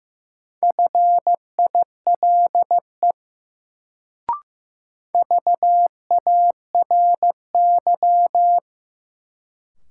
Morse code, heard as a keyed tone, about 15 wpm, 700 Hz. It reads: FILE VARY
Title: Sentences from Top 1200 Words - New Word Drill Alt - 15wpm